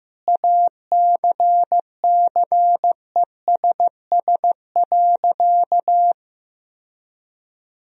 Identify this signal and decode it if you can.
Morse code, heard as a keyed tone, about 15 wpm, 700 Hz. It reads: ACCESS.
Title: Sentences from Top 1000 Words - Review Part 4 - 15wpm